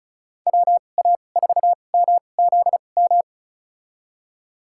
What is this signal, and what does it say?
Morse code, heard as a keyed tone, about 34 wpm, spaced as for 25 wpm, 700 Hz. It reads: WA4MZM
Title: USA Call Signs - 6 Characters - 2x Character Spacing - 35wpm